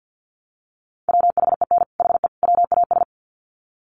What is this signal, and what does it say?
Morse code, heard as a keyed tone, about 50 wpm, 700 Hz. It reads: WHEN HE WAS
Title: Sets of 3 Words 50wpm